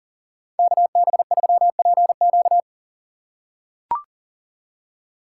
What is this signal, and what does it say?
Morse code, heard as a keyed tone, about 40 wpm, 700 Hz. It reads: KB3PQ
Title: USA Call Signs - 6x Character Spacing - 40wpm